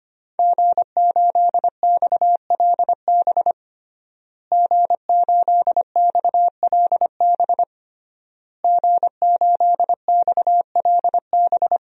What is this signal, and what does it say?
Morse code, heard as a keyed tone, about 25 wpm, 700 Hz. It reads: G8XLB G8XLB G8XLB
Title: DXCC-100 Call Signs - 5 Characters - Repeated 3x - 25wpm